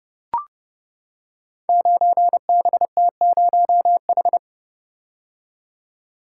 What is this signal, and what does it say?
Morse code, heard as a keyed tone, about 30 wpm, 700 Hz. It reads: 9BT0H
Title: Code Groups - 5 Letter:Numbers - 30wpm